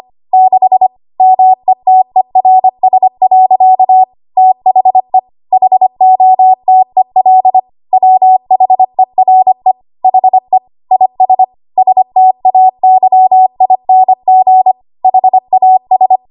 Morse code, heard as a keyed tone, about 25 wpm, 750 Hz. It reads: 6 METERS. THE HOTEL WHERE HE IS STAYING HAS
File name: Archive / 25 WPM / 25WPM